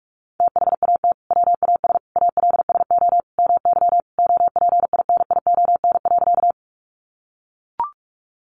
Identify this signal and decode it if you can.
Morse code, heard as a keyed tone, about 44 wpm, 700 Hz. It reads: THAT WAS ALSO MY OPINION.
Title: Sentences from Top 1200 Words 45wpm